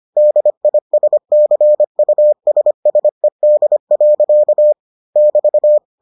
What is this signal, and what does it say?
Morse code, heard as a keyed tone, about 25 wpm, 600 Hz. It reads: DISCUSSED. <BT>